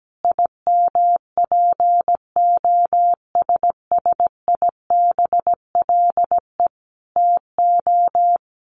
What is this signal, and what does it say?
Morse code, heard as a keyed tone, about 17 wpm, 700 Hz. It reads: IMPOSSIBLE TO
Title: Sentences from Top 1100 Words 17wpm